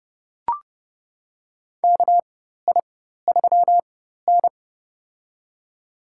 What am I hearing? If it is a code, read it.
Morse code, heard as a keyed tone, about 30 wpm, spaced as for 14 wpm, 700 Hz. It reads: KI3N
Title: USA Call Signs - 4 Characters - 4x Character Spacing - 30wpm